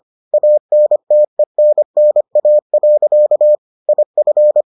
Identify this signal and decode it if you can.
Morse code, heard as a keyed tone, about 25 wpm, 600 Hz. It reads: ANTENNA. IF